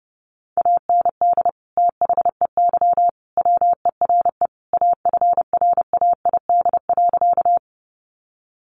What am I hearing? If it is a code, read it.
Morse code, heard as a keyed tone, about 30 wpm, 700 Hz. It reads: AND THEY WERE AFRAID.
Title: Sentences from Top 1500 Words 30wpm